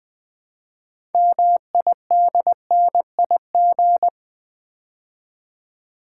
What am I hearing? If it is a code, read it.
Morse code, heard as a keyed tone, about 20 wpm, 700 Hz. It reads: MIDNIG
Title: Build A Word - 3 Letters At A Time - 8-Letter Words - 20wpm